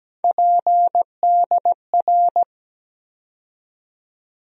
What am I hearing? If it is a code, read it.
Morse code, heard as a keyed tone, about 17 wpm, 700 Hz. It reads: PDR
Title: USA Call Sign Postfixes - 3 Letters 17wpm